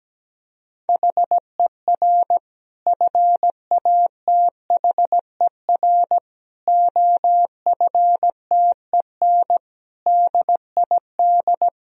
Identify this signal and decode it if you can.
Morse code, heard as a keyed tone, about 17 wpm, 700 Hz. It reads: HER FATHER OFTEN DID
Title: Sentences from Top 1000 Words - Review Part 1 - 17wpm